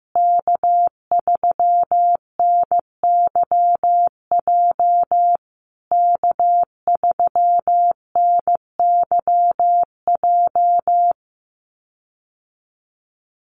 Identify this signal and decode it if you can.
Morse code, heard as a keyed tone, about 15 wpm, 700 Hz. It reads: K3NYJ K3NYJ
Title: USA Call Signs Repeated 3x 15wpm